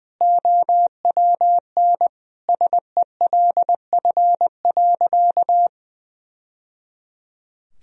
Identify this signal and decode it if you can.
Morse code, heard as a keyed tone, about 21 wpm, 700 Hz. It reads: OWN SELF.
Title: Sentences from Top 1000 Words - Review Part 1 - 20wpm